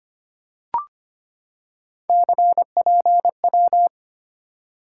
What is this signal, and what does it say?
Morse code, heard as a keyed tone, about 25 wpm, 700 Hz. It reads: CPW
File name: Three Letter-Number Combinations - 25wpm